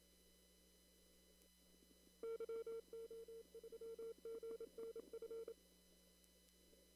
Morse code, heard as a keyed tone, about 28 wpm, 460 Hz. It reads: YO3GNF
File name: CW / kt5j wpx cw-op-k5pi